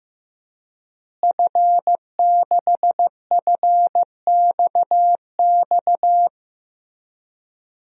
Morse code, heard as a keyed tone, about 15 wpm, 700 Hz. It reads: F6FXX
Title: DXCC-100 Call Signs - 5 Characters - Repeated 3x - 15wpm